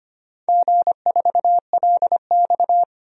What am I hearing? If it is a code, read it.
Morse code, heard as a keyed tone, about 25 wpm, 700 Hz. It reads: G4LX
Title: DXCC-100 Call Signs - 4 Characters - 25wpm